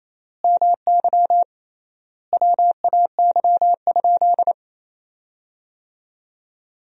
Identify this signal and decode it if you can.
Morse code, heard as a keyed tone, about 28 wpm, 700 Hz. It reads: MY WAY?
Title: 3x Word-Spacing - Sentences from Top 200 Words - 28wpm